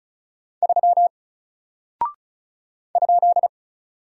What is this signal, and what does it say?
Morse code, heard as a keyed tone, about 36 wpm, 700 Hz. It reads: 3 ?
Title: Single Character (Letter:Number:Punctuation:Prosign) 35wpm